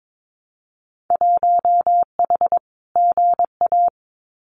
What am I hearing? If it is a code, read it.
Morse code, heard as a keyed tone, about 22 wpm, 700 Hz. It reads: 1H GA
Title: Winter Field Day Exchanges 22wpm